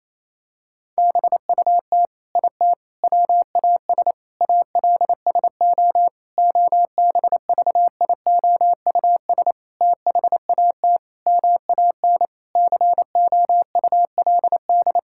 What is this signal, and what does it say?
Morse code, heard as a keyed tone, about 28 wpm, 700 Hz. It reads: BUT IT WAS ALSO OBVIOUS THAT MAN COULD
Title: Sentences from Top 1600 Words 28wpm